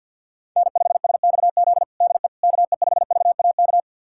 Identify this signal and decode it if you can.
Morse code, heard as a keyed tone, about 46 wpm, 690 Hz. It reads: N5SXZ DE KE5VAK
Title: USA Callsign DE Callsign 50wpm